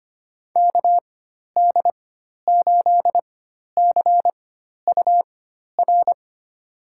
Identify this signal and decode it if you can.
Morse code, heard as a keyed tone, about 25 wpm, spaced as for 12 wpm, 700 Hz. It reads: KD8CUR
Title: USA Call Signs - 6 Characters - 4x Character Spacing - 25wpm